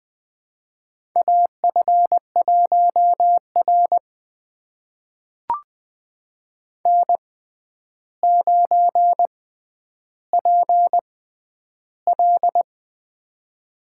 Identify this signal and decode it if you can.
Morse code, heard as a keyed tone, about 20 wpm, spaced as for 10 wpm, 700 Hz. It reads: AF1R N9PL
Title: USA Call Signs - 4 Characters - 6x Character Spacing - 20wpm